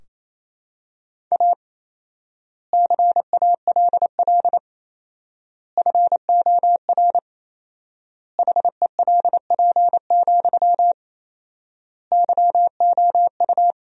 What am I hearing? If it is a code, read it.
Morse code, heard as a keyed tone, about 28 wpm, 700 Hz. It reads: A CALL FOR HELP, YOU
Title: 4x Word-Spacing - Sentences from Top 300 Words - 28wpm